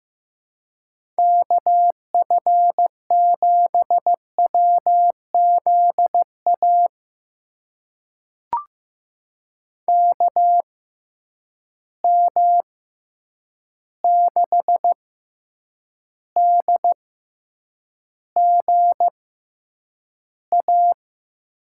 Morse code, heard as a keyed tone, about 15 wpm, spaced as for 8 wpm, 700 Hz. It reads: KF7WZA KM6DGA
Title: USA Call Signs - 6 Characters - 6x Character Spacing - 15wpm